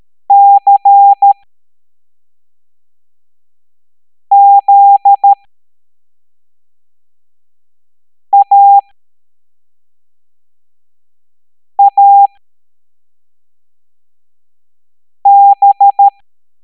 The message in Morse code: CZAAB